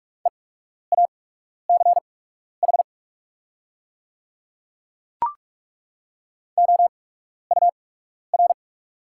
Morse code, heard as a keyed tone, about 43 wpm, spaced as for 12 wpm, 700 Hz. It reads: EACH OUR
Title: 8x Character Spacing - Top 300 Words - 45wpm